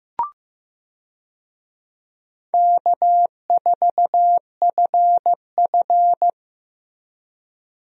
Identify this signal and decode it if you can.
Morse code, heard as a keyed tone, about 16 wpm, 700 Hz. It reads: K4FF
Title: USA Call Signs - 4 Characters - Repeated 3x - 15wpm